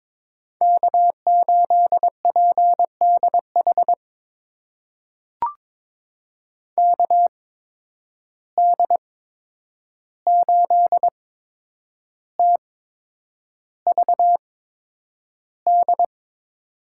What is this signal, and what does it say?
Morse code, heard as a keyed tone, about 22 wpm, spaced as for 9 wpm, 700 Hz. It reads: K8PDH KD8TVD
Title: USA Call Signs - 8x Character Spacing - 22wpm